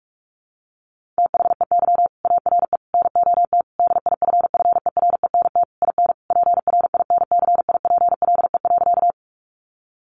Morse code, heard as a keyed tone, about 44 wpm, 700 Hz. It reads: THEY ARE NOT DIFFERENT IN PRINCIPLE.
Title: Sentences from Top 1400 Words 45wpm